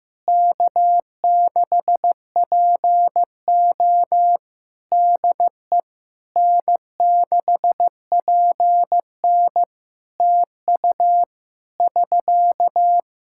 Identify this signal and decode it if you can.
Morse code, heard as a keyed tone, about 15 wpm, 700 Hz. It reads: K6PO DE N6PN TU <SK>
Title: K1USN SST QSO Practice 15wpm